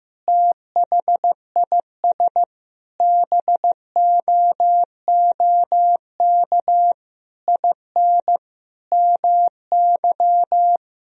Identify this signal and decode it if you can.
Morse code, heard as a keyed tone, about 15 wpm, 700 Hz. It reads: THIS BOOK IN MY
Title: Sentences from Top 1000 Words - Review Part 3 - 15wpm